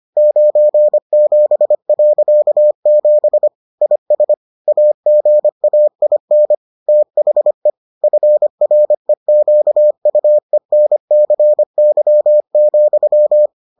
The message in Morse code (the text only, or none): 97.7 IS AGAIN THE FREQUENCY,